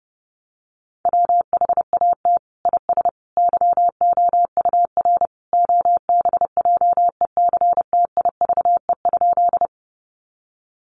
Morse code, heard as a keyed tone, about 30 wpm, 700 Hz. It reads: WHAT IS YOUR OBJECTIVE?
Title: Sentences from Top 1200 Words 30wpm